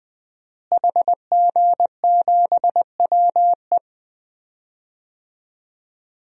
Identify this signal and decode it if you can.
Morse code, heard as a keyed tone, about 20 wpm, 700 Hz. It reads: HG7WE
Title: DXCC-100 Call Signs - 5 Characters - Repeated 2x - 20wpm